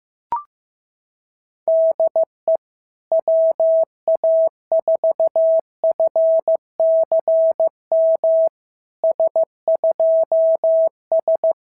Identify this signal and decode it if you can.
Morse code, heard as a keyed tone, about 15 wpm, 690 Hz. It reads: DE WA4FCM S2S
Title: SOTA QSO Practice 15wpm